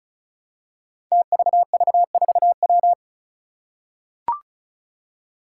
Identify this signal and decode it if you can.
Morse code, heard as a keyed tone, about 36 wpm, 700 Hz. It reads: TVV4W
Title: Five Challenging Letter-Number Combinations 35wpm